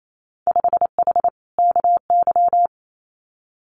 Morse code, heard as a keyed tone, about 28 wpm, 700 Hz. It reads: 5H KY
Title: Winter Field Day Exchanges Encore 28wpm